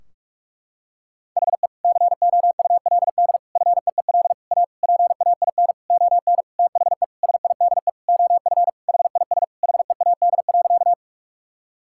Morse code, heard as a keyed tone, about 46 wpm, 700 Hz. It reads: HE COULD FEEL A PAIN ON THE SIDE OF HIS HEAD.
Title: Sentences from Top 1600 Words 45wpm